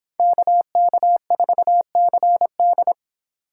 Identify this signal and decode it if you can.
Morse code, heard as a keyed tone, about 25 wpm, 700 Hz. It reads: KK4CD